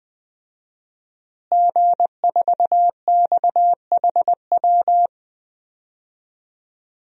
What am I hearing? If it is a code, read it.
Morse code, heard as a keyed tone, about 21 wpm, 700 Hz. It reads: G4XHW
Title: DXCC-100 Call Signs - 5 Characters - Repeated 2x - 20wpm